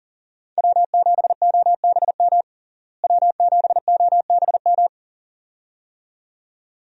Morse code, heard as a keyed tone, about 40 wpm, 700 Hz. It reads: W7OBM W7OBM
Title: USA Call Signs - 5 Characters - Repeated 2x - 40wpm